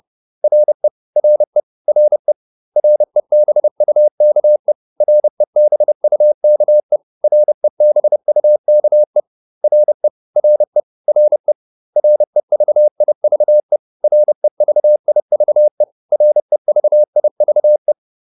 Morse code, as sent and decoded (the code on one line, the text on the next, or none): .-. / .-. / .-. / .-. -... ..- -.- . / .-. -... ..- -.- . / .-. -... ..- -.- . / .-. / .-. / .-. / .-. ...- .. ...- . / .-. ...- .. ...- . / .-. ...- .. ...- .
RE RE RE REBUKE REBUKE REBUKE RE RE RE REVIVE REVIVE REVIVE